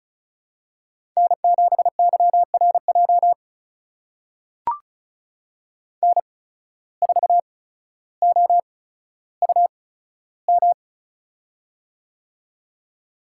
Morse code, heard as a keyed tone, about 36 wpm, spaced as for 15 wpm, 700 Hz. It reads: N7YRJ N4OUM
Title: USA Call Signs - 5 Characters - 8x Character Spacing - 35wpm